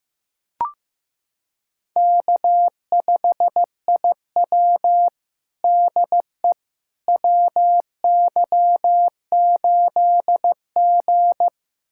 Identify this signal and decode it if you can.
Morse code, heard as a keyed tone, about 15 wpm, 700 Hz. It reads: K5IW DE WY8G
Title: USA Callsign DE Callsign - 4 Characters - 15wpm